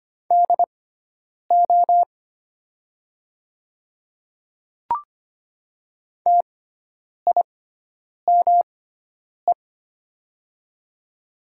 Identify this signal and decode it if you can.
Morse code, heard as a keyed tone, about 25 wpm, spaced as for 9 wpm, 690 Hz. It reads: DO TIME
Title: 6x Character Spacing - Top 100 Words - 25wpm